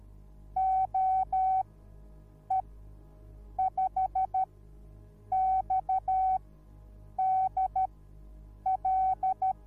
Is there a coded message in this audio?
Yes: Morse code from 0.6 to 9.5 s, heard as a keyed tone, about 13 wpm, spaced as for 7 wpm, 760 Hz. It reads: OE5XDL